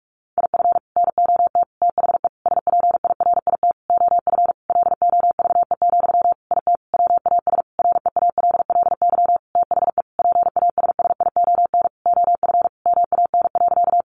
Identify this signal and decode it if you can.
Morse code, heard as a keyed tone, about 44 wpm, 700 Hz. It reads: IF NOT THE SPIRIT OF LOVE, IT WAS REALLY THE PASSION OF MAN.
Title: Sentences from Top 1400 Words 45wpm